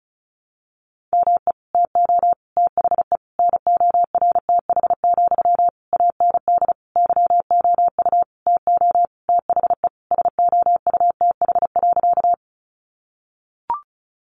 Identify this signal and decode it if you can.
Morse code, heard as a keyed tone, about 34 wpm, 700 Hz. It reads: ME TO THE NORTH, AND YOU TO THE SOUTH.
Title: Sentences from Top 400 Words 35wpm